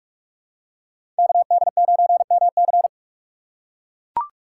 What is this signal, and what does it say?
Morse code, heard as a keyed tone, about 42 wpm, 700 Hz. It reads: KD9MC
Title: USA Call Signs - 5 Characters - 2x Character Spacing - 45wpm